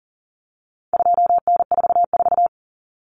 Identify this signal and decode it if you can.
Morse code, heard as a keyed tone, about 40 wpm, 700 Hz. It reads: 2N44
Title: Common Cut Numbers - 4-Digit 40wpm